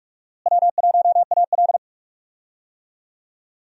Morse code, heard as a keyed tone, about 46 wpm, 700 Hz. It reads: W1AL
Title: USA Call Signs - 4 Characters - Repeated 2x - 45wpm